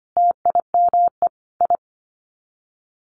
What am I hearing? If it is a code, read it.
Morse code, heard as a keyed tone, about 25 wpm, 700 Hz. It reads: TIME I
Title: Sets of 5 Words Encore 25wpm